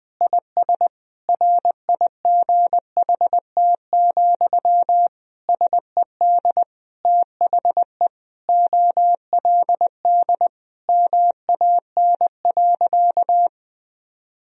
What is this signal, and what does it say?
Morse code, heard as a keyed tone, about 20 wpm, 700 Hz. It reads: IS RIGHT, SED THE OLD MAN.
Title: Sentences from Top 200 Words and Abbreviations 20wpm